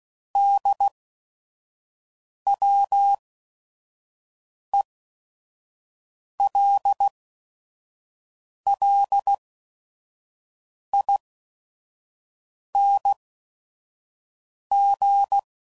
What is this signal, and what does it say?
Morse code, heard as a keyed tone, about 16 wpm, spaced as for 5 wpm, 800 Hz. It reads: DWELLING
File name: Suffixes / Special ING's / VerySlowToFaster ING's